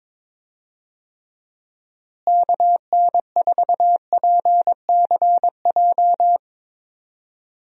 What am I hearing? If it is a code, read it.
Morse code, heard as a keyed tone, about 21 wpm, 700 Hz. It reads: KN4PCJ
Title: USA Call Signs - 6 Characters - Repeated 3x - 22wpm